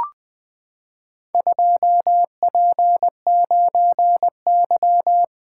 Morse code, heard as a keyed tone, about 20 wpm, 700 Hz. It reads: 2P9Y
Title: Four Letter-Number Combinations - 20wpm